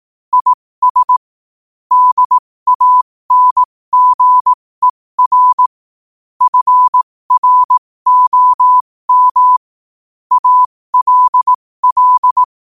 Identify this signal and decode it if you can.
Morse code, heard as a keyed tone, about 18 wpm, spaced as for 15 wpm, 1000 Hz. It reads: IS DANGER FROM ALL
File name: qotdfiles / 15wpm